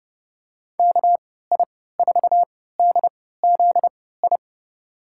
Morse code, heard as a keyed tone, about 32 wpm, spaced as for 17 wpm, 700 Hz. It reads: KI4DZI
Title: USA Call Signs - 6 Characters - 3x Character Spacing - 30wpm